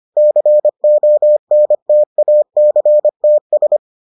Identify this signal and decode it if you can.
Morse code, heard as a keyed tone, about 25 wpm, 600 Hz. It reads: CONTACTS